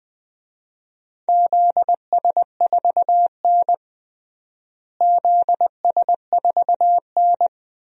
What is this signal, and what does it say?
Morse code, heard as a keyed tone, about 20 wpm, 700 Hz. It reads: ZS4N ZS4N
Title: DXCC-100 Call Signs - 4 Characters - Repeated 3x - 20wpm